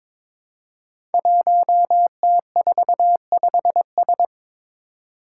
Morse code, heard as a keyed tone, about 22 wpm, 700 Hz. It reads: 1T45S
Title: Five Challenging Letter-Number Combinations 22wpm